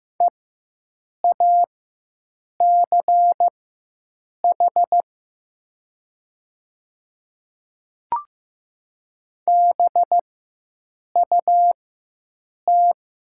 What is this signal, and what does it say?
Morse code, heard as a keyed tone, about 15 wpm, spaced as for 7 wpm, 700 Hz. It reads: EACH BUT
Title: 4x Character Spacing - Top 200 Words - 15wpm